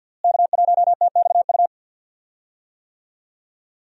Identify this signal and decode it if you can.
Morse code, heard as a keyed tone, about 52 wpm, 700 Hz. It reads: K1TXU
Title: USA Call Signs - 5 Characters - Repeated 2x - 50wpm